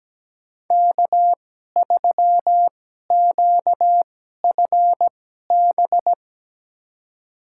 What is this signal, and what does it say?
Morse code, heard as a keyed tone, about 17 wpm, spaced as for 12 wpm, 700 Hz. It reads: K3QFB
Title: USA Call Signs - 2x Character Spacing - 17wpm